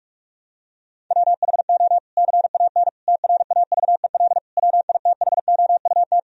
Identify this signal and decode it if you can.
Morse code, heard as a keyed tone, about 46 wpm, 700 Hz. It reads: WHO CAN TRAVEL WITHOUT
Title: Sentences from Top 1000 Words - Review Part 3 - 45wpm